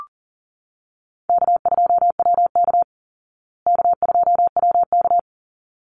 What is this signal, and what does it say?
Morse code, heard as a keyed tone, about 38 wpm, 700 Hz. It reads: K2WK K2WK
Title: Call Signs Repeated 3x - 40wpm